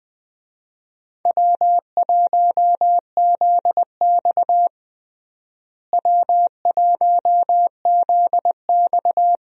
Morse code, heard as a keyed tone, about 20 wpm, 700 Hz. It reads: W1ZX W1ZX
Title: USA Call Signs - 4 Characters - Repeated 2x - 20wpm